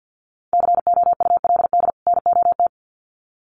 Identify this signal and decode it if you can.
Morse code, heard as a keyed tone, about 50 wpm, 700 Hz. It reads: COULD NOT
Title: Sets of 2 Words 50wpm